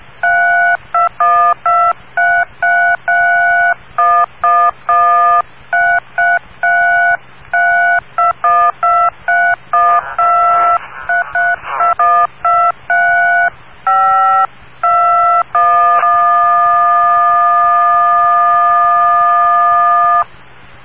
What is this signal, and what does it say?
DTMF, heard as telephone touch tones, tones of about 280 ms. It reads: 32123331113333212312332123521